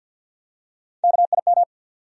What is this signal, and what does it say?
Morse code, heard as a keyed tone, about 52 wpm, 700 Hz. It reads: KIM
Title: USA Common Names - Max 3 Letters - 50wpm